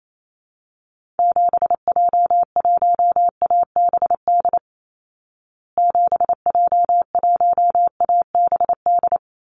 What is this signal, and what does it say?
Morse code, heard as a keyed tone, about 28 wpm, 700 Hz. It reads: 7J1ABD 7J1ABD
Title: Call Signs Repeated 3x - 28wpm